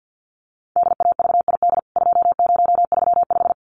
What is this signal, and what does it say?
Morse code, heard as a keyed tone, about 50 wpm, 700 Hz. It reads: DAVID 2035
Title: CWT Exchange 50wpm